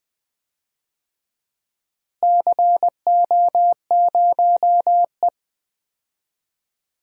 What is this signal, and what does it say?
Morse code, heard as a keyed tone, about 20 wpm, 700 Hz. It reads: CO0E
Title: DXCC-100 Call Signs - 4 Characters - Repeated 3x - 20wpm